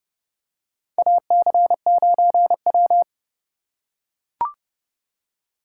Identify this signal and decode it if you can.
Morse code, heard as a keyed tone, about 30 wpm, 700 Hz. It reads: AC9W
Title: USA Call Signs - 4 Characters - 3x Character Spacing - 30wpm